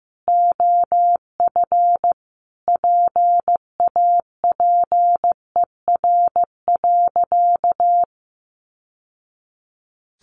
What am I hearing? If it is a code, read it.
Morse code, heard as a keyed tone, about 15 wpm, 700 Hz. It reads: OF PAPER.
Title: Sentences from Top 900 Words 15wpm